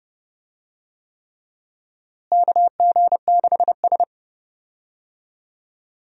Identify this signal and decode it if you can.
Morse code, heard as a keyed tone, about 30 wpm, 700 Hz. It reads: KG6S
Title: USA Call Signs - 4 Characters - Repeated 3x - 30wpm